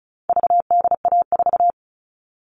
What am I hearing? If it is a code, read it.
Morse code, heard as a keyed tone, about 34 wpm, 700 Hz. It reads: VDA4
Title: All Cut Numbers - 4-Digit 35wpm